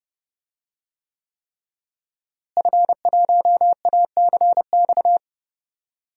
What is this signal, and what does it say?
Morse code, heard as a keyed tone, about 32 wpm, 700 Hz. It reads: F1ACX
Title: DXCC-100 Call Signs - 5 Characters - Repeated 3x - 30wpm